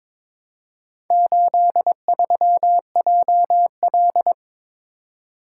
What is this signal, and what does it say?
Morse code, heard as a keyed tone, about 22 wpm, 700 Hz. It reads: 83JL
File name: practice / 22wpm / Four Letter-Number Combinations - 22wpm